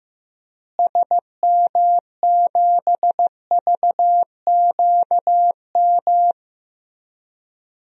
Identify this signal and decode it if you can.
Morse code, heard as a keyed tone, about 15 wpm, 700 Hz. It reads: SM7VQM
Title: DXCC-100 Call Signs - 6 Characters - Repeated 2x - 15wpm